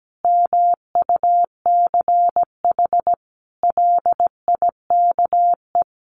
Morse code, heard as a keyed tone, about 17 wpm, 700 Hz. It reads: MUCH LIKE
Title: Sentences from Top 1300 Words 17wpm